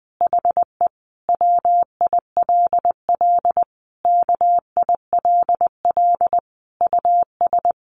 Morse code, heard as a keyed tone, about 19 wpm, 700 Hz. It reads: HE WILL KILL US
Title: Sentences from Top 1400 Words 20wpm